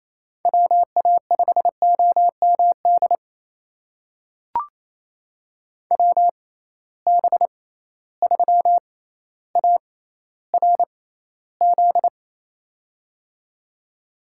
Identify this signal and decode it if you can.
Morse code, heard as a keyed tone, about 28 wpm, spaced as for 14 wpm, 700 Hz. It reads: WA5OMD WB3ARZ
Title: USA Call Signs - 6 Characters - 6x Character Spacing - 28wpm